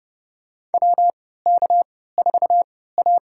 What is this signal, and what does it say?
Morse code, heard as a keyed tone, about 30 wpm, spaced as for 17 wpm, 700 Hz. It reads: WK4A
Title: USA Call Signs - 4 Characters - 3x Character Spacing - 30wpm